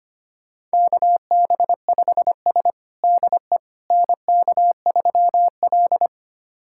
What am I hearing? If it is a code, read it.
Morse code, heard as a keyed tone, about 25 wpm, 700 Hz. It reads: KB5S DE NK3L
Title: USA Callsign DE Callsign - 4 Characters - 25wpm